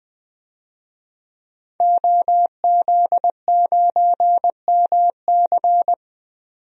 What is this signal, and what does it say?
Morse code, heard as a keyed tone, about 20 wpm, 700 Hz. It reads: OZ9MC
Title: DXCC-100 Call Signs - 5 Characters - 20wpm